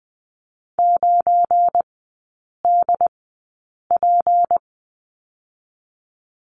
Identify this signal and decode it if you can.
Morse code, heard as a keyed tone, about 20 wpm, spaced as for 8 wpm, 700 Hz. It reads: 9DP
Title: 205 Lesson 33e - Letter j - Three Characters 20wpm